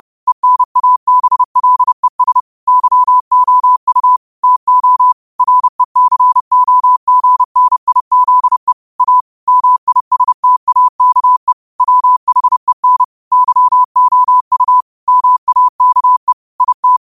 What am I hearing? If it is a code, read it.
Morse code, heard as a keyed tone, about 30 wpm, 1000 Hz. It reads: ENABLES YOU TO RECOGNIZE A MISTAKE WHEN YOU MAKE IT